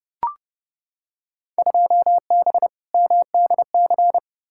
Morse code, heard as a keyed tone, about 29 wpm, 700 Hz. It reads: 2B MDC
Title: Field Day Exchanges - 30wpm